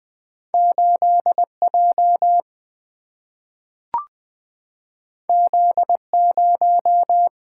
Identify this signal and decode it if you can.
Morse code, heard as a keyed tone, about 20 wpm, 700 Hz. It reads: 8 J Z 0
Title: Single-Letter Single-Number 20wpm